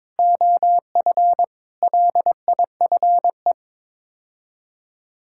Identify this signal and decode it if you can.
Morse code, heard as a keyed tone, about 22 wpm, 700 Hz. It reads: OF LIFE
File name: practice / 22wpm / Sets of 6 Words - 22wpm